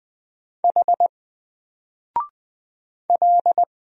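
Morse code, heard as a keyed tone, about 20 wpm, 700 Hz. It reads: H L
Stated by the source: Single letter and number Morse drill at 20 wpm, from N4LCD.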